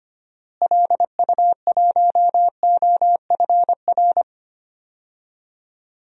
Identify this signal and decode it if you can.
Morse code, heard as a keyed tone, about 25 wpm, 700 Hz. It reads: LU1OFR
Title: DXCC-100 Call Signs - 6 Characters - Repeated 3x - 25wpm